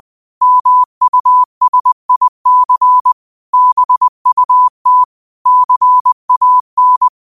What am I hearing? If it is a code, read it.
Morse code, heard as a keyed tone, about 20 wpm, 1000 Hz. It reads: MUSIC BUT CAN